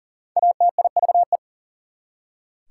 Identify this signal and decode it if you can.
Morse code, heard as a keyed tone, about 40 wpm, 700 Hz. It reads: ATIVE
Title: Prefixes and Suffixes 40wpm